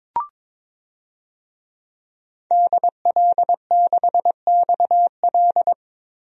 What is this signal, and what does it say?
Morse code, heard as a keyed tone, about 22 wpm, 700 Hz. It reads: DL6XL
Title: DXCC-100 Call Signs - 5 Characters - 22wpm